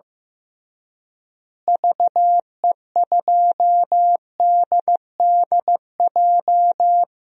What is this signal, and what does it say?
Morse code, heard as a keyed tone, about 15 wpm, 700 Hz. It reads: VE2DDJ
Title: DXCC-100 Call Signs - 6 Characters - Repeated 3x - 15wpm